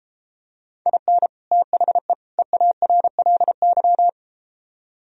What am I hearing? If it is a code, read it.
Morse code, heard as a keyed tone, about 33 wpm, 700 Hz. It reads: IN THE EARLY